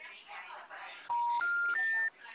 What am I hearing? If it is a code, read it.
Special information tone, with speech.